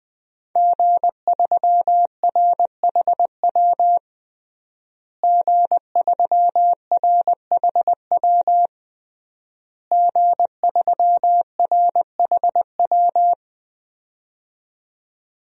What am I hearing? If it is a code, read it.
Morse code, heard as a keyed tone, about 21 wpm, 700 Hz. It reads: G3RHW G3RHW G3RHW
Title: DXCC-100 Call Signs - 5 Characters - Repeated 3x - 20wpm